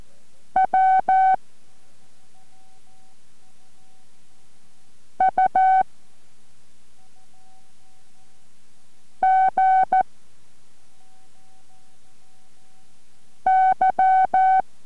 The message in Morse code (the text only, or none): W U G Y